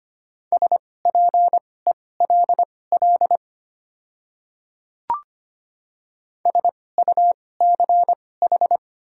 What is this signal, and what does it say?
Morse code, heard as a keyed tone, about 25 wpm, spaced as for 18 wpm, 700 Hz. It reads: SPELL SUCH
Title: 2x Character Spacing - Top 200 Words - 25wpm